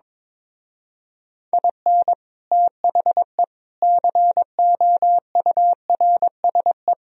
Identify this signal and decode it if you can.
Morse code, heard as a keyed tone, about 22 wpm, 700 Hz. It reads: IN THE COURSE
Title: Speed-Racing - Sets of 5 Words 22wpm